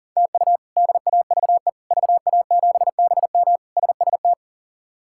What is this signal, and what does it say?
Morse code, heard as a keyed tone, about 37 wpm, 700 Hz. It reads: TU DAVE VA7BM SST
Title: K1USN SST QSO Practice - 40wpm